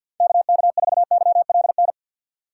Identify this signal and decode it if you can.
Morse code, heard as a keyed tone, about 51 wpm, 700 Hz. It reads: KK4YLN
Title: USA Call Signs - 6 Characters - 6x Character Spacing - 50wpm